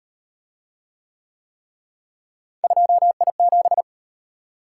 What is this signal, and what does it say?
Morse code, heard as a keyed tone, about 35 wpm, 700 Hz. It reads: 2I7